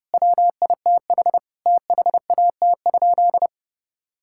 Morse code, heard as a keyed tone, about 30 wpm, 700 Hz. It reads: WITH THAT?
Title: Sentences from Top 1000 Words - Review Part 6 - 30wpm